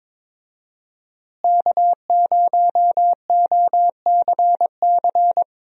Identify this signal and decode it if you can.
Morse code, heard as a keyed tone, about 22 wpm, 700 Hz. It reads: K0OCC
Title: USA Call Signs - 5 Characters - Repeated 2x - 22wpm